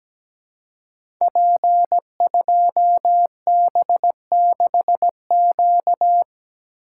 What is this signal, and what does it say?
Morse code, heard as a keyed tone, about 17 wpm, 700 Hz. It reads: P2B6Q
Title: Code Groups - 5 Letter:Numbers - 17wpm